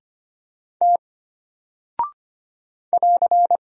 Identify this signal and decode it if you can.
Morse code, heard as a keyed tone, about 25 wpm, 700 Hz. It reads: T <AR>
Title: Single Character (Letter:Number:Punctuation:Prosign) Alt 25wpm